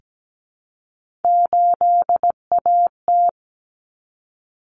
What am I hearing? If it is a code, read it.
Morse code, heard as a keyed tone, about 17 wpm, 700 Hz. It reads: 8AT
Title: Common Cut Numbers - 3-Digit 17wpm